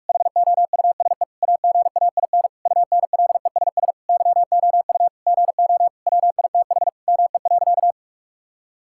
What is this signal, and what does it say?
Morse code, heard as a keyed tone, about 48 wpm, 700 Hz. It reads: HOUSE AGAIN UNLESS YOU GO WITH ME.
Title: Sentences from Top 1000 Words - Review Part 6 - 45wpm